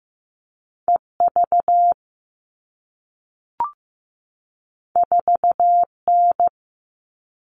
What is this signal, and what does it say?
Morse code, heard as a keyed tone, about 15 wpm, 700 Hz. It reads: EV 4N
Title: All Cut Numbers - 2-Digit 15wpm